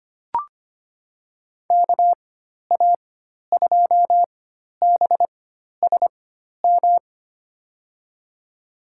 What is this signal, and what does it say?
Morse code, heard as a keyed tone, about 25 wpm, spaced as for 12 wpm, 700 Hz. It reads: KA2BSM
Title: USA Call Signs - 6 Characters - 4x Character Spacing - 25wpm